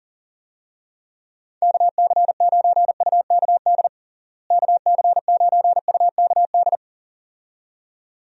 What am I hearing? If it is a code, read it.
Morse code, heard as a keyed tone, about 40 wpm, 700 Hz. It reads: KC9UKD KC9UKD
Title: USA Call Signs - 6 Characters - Repeated 2x - 40wpm